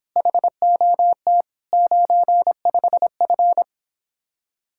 Morse code, heard as a keyed tone, about 31 wpm, 700 Hz. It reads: HOT 95F